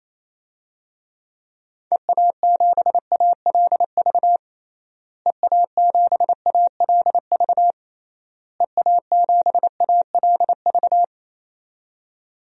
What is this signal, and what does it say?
Morse code, heard as a keyed tone, about 28 wpm, 700 Hz. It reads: EA7ALV EA7ALV EA7ALV
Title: DXCC-100 Call Signs - 6 Characters - Repeated 3x - 28wpm